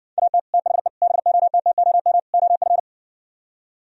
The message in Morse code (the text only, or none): AT THE BOTTOM OF